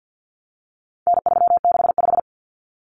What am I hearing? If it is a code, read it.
Morse code, heard as a keyed tone, about 50 wpm, 700 Hz. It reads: N365
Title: Common Cut Numbers - 4-Digit 50wpm